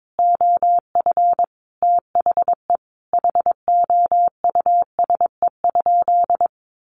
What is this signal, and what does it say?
Morse code, heard as a keyed tone, about 22 wpm, 700 Hz. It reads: OF THE HOUSE?
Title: Sentences from Top 200 Words 22wpm